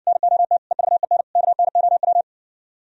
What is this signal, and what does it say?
Morse code, heard as a keyed tone, about 59 wpm, 700 Hz. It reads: NOT EVEN KNOW